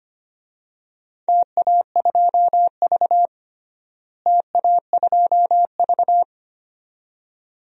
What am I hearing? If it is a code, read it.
Morse code, heard as a keyed tone, about 25 wpm, 700 Hz. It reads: TA2V TA2V
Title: DXCC-100 Call Signs - 4 Characters - Repeated 2x - 25wpm